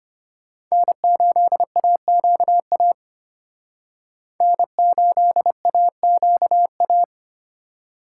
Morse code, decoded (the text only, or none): N8AQA N8AQA